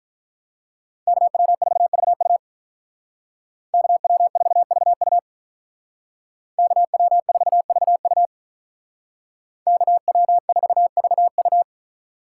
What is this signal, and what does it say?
Morse code, heard as a keyed tone, about 46 wpm, 700 Hz. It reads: KW4VU KW4VU KW4VU KW4VU